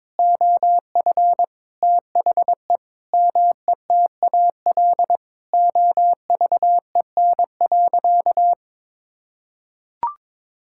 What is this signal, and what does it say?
Morse code, heard as a keyed tone, about 22 wpm, 700 Hz. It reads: OF THE METAL OVEN.
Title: Sentences from Top 1000 Words - Review Part 6 - 22wpm